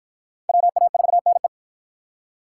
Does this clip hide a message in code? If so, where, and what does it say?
Yes: Morse code from 0.5 to 1.5 s, heard as a keyed tone, about 53 wpm, 700 Hz. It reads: WA4NE